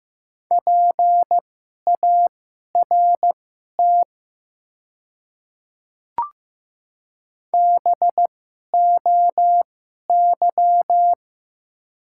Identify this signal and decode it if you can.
Morse code, heard as a keyed tone, about 15 wpm, spaced as for 11 wpm, 700 Hz. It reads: PART BOY
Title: 2x Character Spacing - Top 300 Words - 15wpm